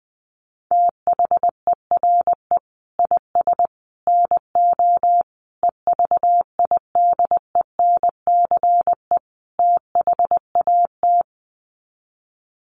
Morse code, heard as a keyed tone, about 19 wpm, 700 Hz. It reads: THERE IS NO EVIDENCE THAT
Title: Sets of 5 Words Encore 20wpm